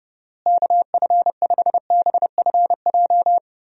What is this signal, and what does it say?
Morse code, heard as a keyed tone, about 32 wpm, 700 Hz. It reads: KF5BFJ
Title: USA Call Signs - 6 Characters - Repeated 3x - 30wpm